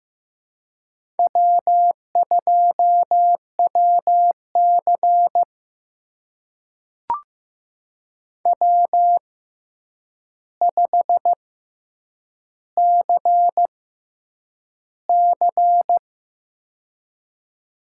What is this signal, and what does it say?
Morse code, heard as a keyed tone, about 15 wpm, spaced as for 8 wpm, 700 Hz. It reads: W2WC W5CC
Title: USA Call Signs - 4 Characters - 6x Character Spacing - 15wpm